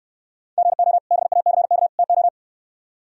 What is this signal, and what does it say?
Morse code, heard as a keyed tone, about 66 wpm, 700 Hz. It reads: GO BACK TO